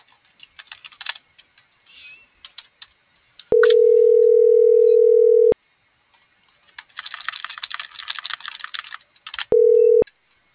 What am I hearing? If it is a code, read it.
Ringback tone (North America), 440 + 480 Hz.
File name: CENTRAL GENERAL / out michelle.carrera